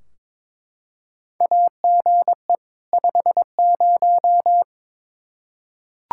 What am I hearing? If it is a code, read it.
Morse code, heard as a keyed tone, about 22 wpm, 700 Hz. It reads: AGE 50
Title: K1USN SST QSO Practice 22wpm